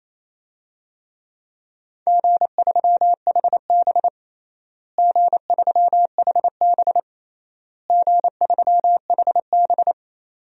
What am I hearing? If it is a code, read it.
Morse code, heard as a keyed tone, about 27 wpm, 700 Hz. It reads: G3HB G3HB G3HB
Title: DXCC-100 Call Signs - 4 Characters - Repeated 3x - 28wpm